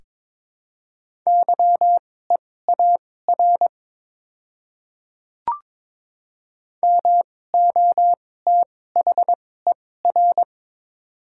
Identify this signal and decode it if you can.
Morse code, heard as a keyed tone, about 23 wpm, spaced as for 16 wpm, 700 Hz. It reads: YEAR MOTHER
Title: 2x Character Spacing - Top 200 Words - 22wpm